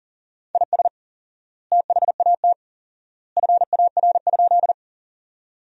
Morse code, heard as a keyed tone, about 40 wpm, 700 Hz. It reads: IS THAT FAR?
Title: 4x Word-Spacing - Sentences from Top 300 Words - 40wpm